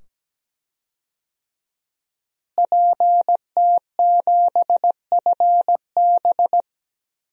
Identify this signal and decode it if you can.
Morse code, heard as a keyed tone, about 17 wpm, 700 Hz. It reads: PT7FB
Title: DXCC-100 Call Signs - 5 Characters - Repeated 2x - 17wpm